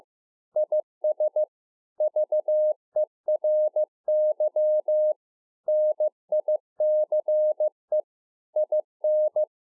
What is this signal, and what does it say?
Morse code, heard as a keyed tone, about 15 wpm, 610 Hz. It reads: IS VERY NICE IN
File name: 15WPM